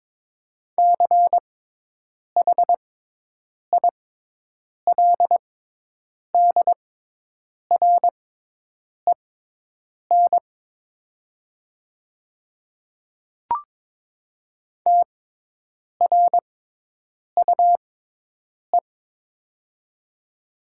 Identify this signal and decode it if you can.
Morse code, heard as a keyed tone, about 22 wpm, spaced as for 8 wpm, 700 Hz. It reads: CHILDREN TRUE
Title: 6x Character Spacing - Top 500 Words - 22wpm